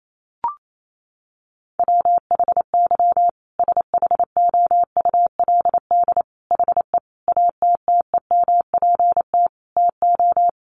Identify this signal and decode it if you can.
Morse code, heard as a keyed tone, about 28 wpm, 700 Hz. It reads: WHY SHOULD HE ATTEMPT TO
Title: Sentences from Top 1400 Words 28wpm